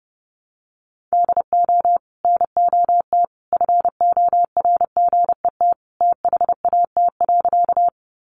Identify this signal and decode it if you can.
Morse code, heard as a keyed tone, about 30 wpm, 700 Hz. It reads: DO NOT FORGET THAT.
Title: Sentences from Top 1200 Words 30wpm